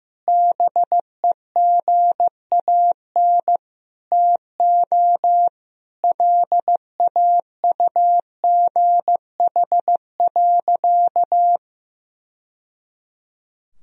Morse code, heard as a keyed tone, about 16 wpm, 700 Hz. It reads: BEGAN TO LAUGH.
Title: Sentences from Top 1000 Words - Review Part 2 - 15wpm